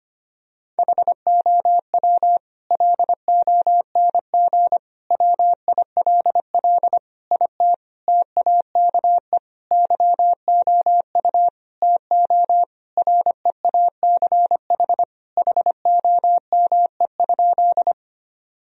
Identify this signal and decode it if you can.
Morse code, heard as a keyed tone, about 25 wpm, 700 Hz. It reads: HOW LONG WILL IT TAKE YOU TO REACH HOME?
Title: Sentences from Top 1000 Words - Review Part 2 - 25wpm